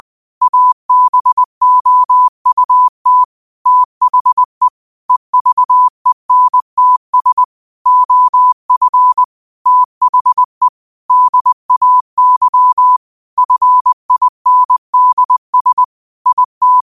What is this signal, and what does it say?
Morse code, heard as a keyed tone, about 20 wpm, 1000 Hz. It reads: ABOUT THE EVENTS OF THE DAY FINDS IT